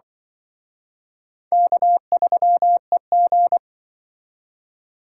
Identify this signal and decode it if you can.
Morse code, heard as a keyed tone, about 25 wpm, 700 Hz. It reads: K3EG
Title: Speed Racing - USA Call Signs 20wpm